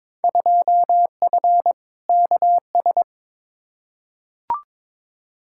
Morse code, heard as a keyed tone, about 22 wpm, 700 Hz. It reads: 2F KS
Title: Field Day Exchanges - 22wpm